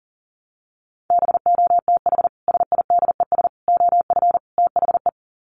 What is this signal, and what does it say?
Morse code, heard as a keyed tone, about 40 wpm, 700 Hz. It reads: BOTH SIDES OF THE
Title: Sets of 4 Words 40wpm